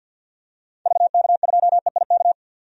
Morse code, heard as a keyed tone, about 52 wpm, 700 Hz. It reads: VK2EIK
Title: DXCC-100 Call Signs - 6 Characters - 50wpm